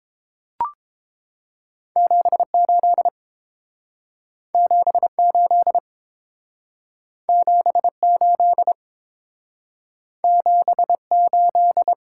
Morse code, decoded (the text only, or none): 78 78 78 78